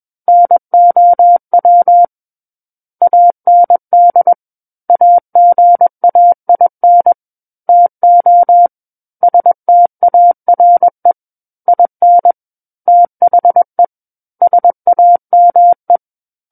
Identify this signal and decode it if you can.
Morse code, heard as a keyed tone, about 21 wpm, 700 Hz. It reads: NOW AND AGAIN TO STARE IN THE SAME
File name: Book One / 12-Chapter 12 - 21 WPM